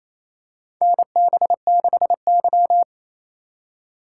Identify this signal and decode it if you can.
Morse code, heard as a keyed tone, about 27 wpm, 700 Hz. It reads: NB6Y
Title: USA Call Signs - 4 Characters - 6x Character Spacing - 28wpm